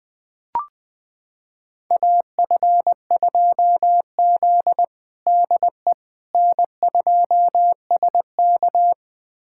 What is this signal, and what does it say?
Morse code, heard as a keyed tone, about 20 wpm, 700 Hz. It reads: AF2Z DE N2SK
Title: USA Callsign DE Callsign - 4 Characters - 20wpm